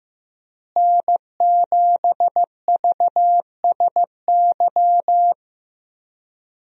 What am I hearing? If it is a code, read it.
Morse code, heard as a keyed tone, about 15 wpm, 700 Hz. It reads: N7VSY
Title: USA Call Signs - 5 Characters - Repeated 3x - 15wpm